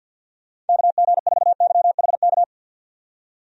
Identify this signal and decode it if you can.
Morse code, heard as a keyed tone, about 51 wpm, 700 Hz. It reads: KG4YHK
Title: USA Call Signs - 6 Characters - 50wpm